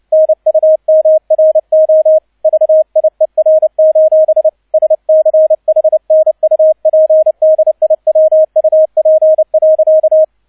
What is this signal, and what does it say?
Morse code, heard as a keyed tone, about 29 wpm, 630 Hz, with radio noise. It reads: NUMRO VIER: SCHNUPDIWUP.
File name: 002 Zweiter Streich 30wpm